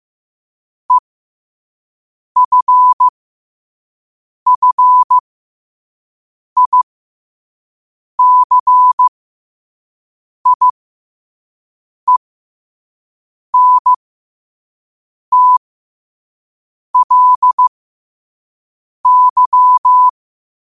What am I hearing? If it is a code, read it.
Morse code, heard as a keyed tone, about 15 wpm, spaced as for 5 wpm, 1000 Hz. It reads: EFFICIENTLY
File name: qotdfiles / 5wpm